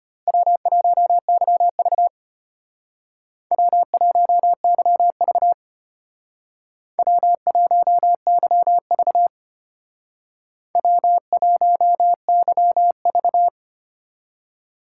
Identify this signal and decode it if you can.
Morse code, heard as a keyed tone, about 30 wpm, 700 Hz. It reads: W1YV W1YV W1YV W1YV
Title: Speed Racing - USA 1x2 Call Signs 25wpm